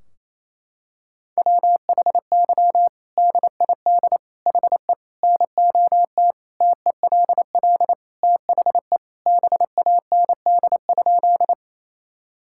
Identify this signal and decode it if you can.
Morse code, heard as a keyed tone, about 28 wpm, 700 Hz. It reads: WHY DID HE NOT TELL THE BAND?
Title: Sentences from Top 1600 Words 28wpm